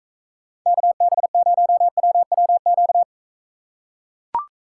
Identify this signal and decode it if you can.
Morse code, heard as a keyed tone, about 43 wpm, 700 Hz. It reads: KB0WWQ